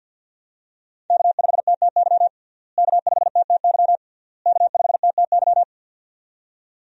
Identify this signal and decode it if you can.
Morse code, heard as a keyed tone, about 51 wpm, 700 Hz. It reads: K5TTY K5TTY K5TTY
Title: USA Call Signs - 5 Characters - Repeated 3x - 50wpm